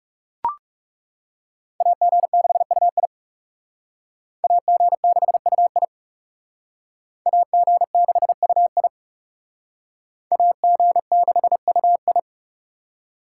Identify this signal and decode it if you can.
Morse code, heard as a keyed tone, about 38 wpm, 700 Hz. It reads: AG6UI AG6UI AG6UI AG6UI
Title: Speed Racing - USA Call Signs 30wpm